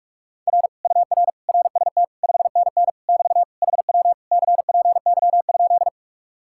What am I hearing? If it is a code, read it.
Morse code, heard as a keyed tone, about 37 wpm, 700 Hz. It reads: R UR RST 5NN <BT> HW CPY?
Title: Speed-Racing Common QSO Phrases - 30wpm